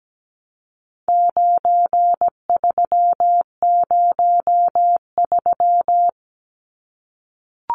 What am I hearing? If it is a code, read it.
Morse code, heard as a keyed tone, about 17 wpm, 700 Hz. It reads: 9303
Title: Numbers - 4-Digit 17wpm